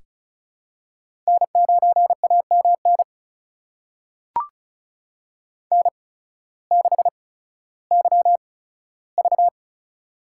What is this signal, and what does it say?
Morse code, heard as a keyed tone, about 36 wpm, spaced as for 16 wpm, 700 Hz. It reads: N9AMN N6YV
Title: USA Call Signs - 8x Character Spacing - 35wpm